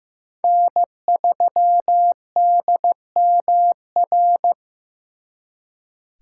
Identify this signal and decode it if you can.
Morse code, heard as a keyed tone, about 18 wpm, 700 Hz. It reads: N3DMR
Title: Speed Racing - USA 1x3 Call Signs 15wpm